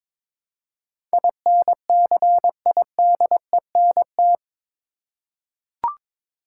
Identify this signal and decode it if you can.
Morse code, heard as a keyed tone, about 22 wpm, 700 Hz. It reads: INCIDENT
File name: practice / 22wpm / Eight Letter Words - 22wpm